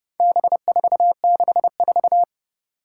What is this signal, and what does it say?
Morse code, heard as a keyed tone, about 30 wpm, 700 Hz. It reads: B464
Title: Four Challenging Letter-Number Combinations 30wpm